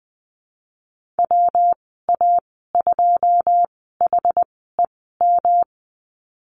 Morse code, heard as a keyed tone, about 19 wpm, spaced as for 14 wpm, 700 Hz. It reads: WA2HEM
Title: 147 Lesson 24g - Letter y - Call Signs 20wpm